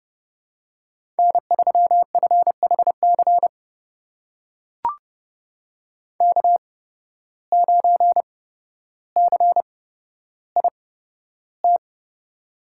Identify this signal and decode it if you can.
Morse code, heard as a keyed tone, about 32 wpm, spaced as for 13 wpm, 700 Hz. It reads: N3FHC K9CIT
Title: USA Call Signs - 5 Characters - 8x Character Spacing - 30wpm